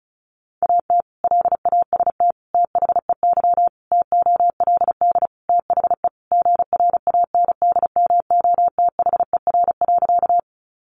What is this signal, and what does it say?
Morse code, heard as a keyed tone, about 34 wpm, 700 Hz. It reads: AT LAST THEY TOLD THE GRANDMOTHER.
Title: Sentences from Top 1200 Words 35wpm